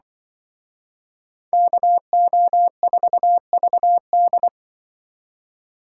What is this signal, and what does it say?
Morse code, heard as a keyed tone, about 25 wpm, 700 Hz. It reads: KO4VD
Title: Speed Racing - USA 2x2 Call Signs 20wpm